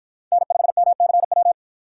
Morse code, heard as a keyed tone, about 48 wpm, 700 Hz. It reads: N5MCW